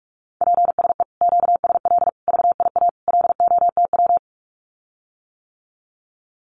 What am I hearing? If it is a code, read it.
Morse code, heard as a keyed tone, about 46 wpm, 700 Hz. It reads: PSE QSL VIA LOTW
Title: Common QSO Phrases Encore 45wpm